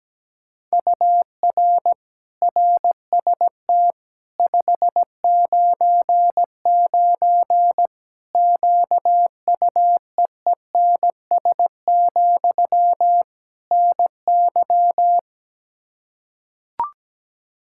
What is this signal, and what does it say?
Morse code, heard as a keyed tone, about 17 wpm, 700 Hz. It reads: UR RST 599 QUEENS, NY
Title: Practice QSO - Exchange 1 - Beginning - 17wpm